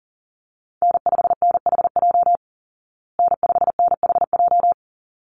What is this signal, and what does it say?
Morse code, heard as a keyed tone, about 40 wpm, 700 Hz. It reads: N5NHJ N5NHJ
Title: Call Signs Repeated 3x - 40wpm